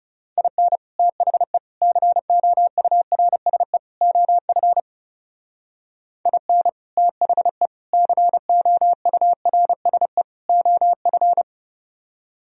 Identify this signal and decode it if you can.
Morse code, heard as a keyed tone, about 39 wpm, 700 Hz. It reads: IN THE COURSE OF IN THE COURSE OF